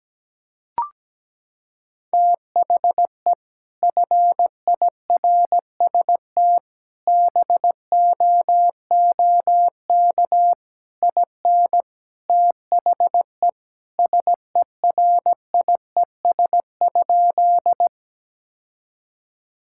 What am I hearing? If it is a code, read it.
Morse code, heard as a keyed tone, about 17 wpm, 700 Hz. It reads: THE FIRST BOOK IN THE SERIES?
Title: Sentences from Top 1000 Words - Review Part 5 - 17wpm